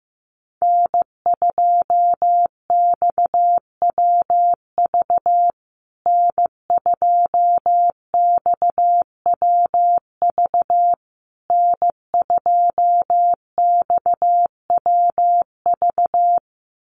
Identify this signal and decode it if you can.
Morse code, heard as a keyed tone, about 15 wpm, 700 Hz. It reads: N2XWV N2XWV N2XWV
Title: USA Call Signs Repeated 3x 15wpm